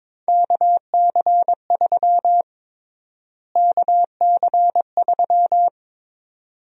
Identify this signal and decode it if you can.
Morse code, heard as a keyed tone, about 22 wpm, 700 Hz. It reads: KC3 KC3
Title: USA Call Sign Prefixes - Repeated 2x - 22wpm